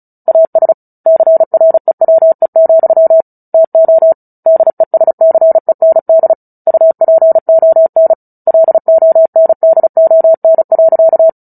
Morse code, heard as a keyed tone, about 34 wpm, 650 Hz. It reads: AS CREWE, TO DESCEND UPON LONDON.
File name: Book Two / 09-Chapter 9 - 35 WPM